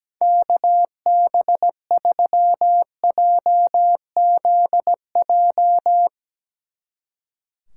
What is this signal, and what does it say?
Morse code, heard as a keyed tone, about 17 wpm, 700 Hz. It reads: KB3JZJ
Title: Speed Racing - USA 2x3 Call Signs 17wpm